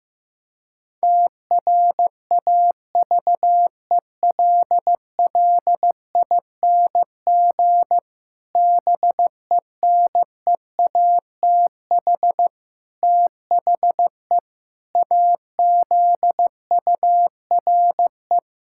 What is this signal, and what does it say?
Morse code, heard as a keyed tone, about 15 wpm, 700 Hz. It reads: TRAVELLING BENEATH THE AZURE
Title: Pangram Challenge v2 - 15wpm